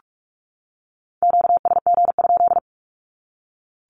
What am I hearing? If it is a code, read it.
Morse code, heard as a keyed tone, about 44 wpm, 700 Hz. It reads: QSG?
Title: Q-Codes 45wpm